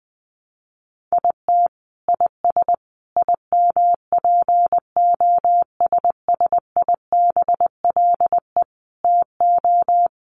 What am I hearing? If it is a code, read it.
Morse code, heard as a keyed tone, about 19 wpm, 700 Hz. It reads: IT IS IMPOSSIBLE TO
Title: Sentences from Top 1200 Words 20wpm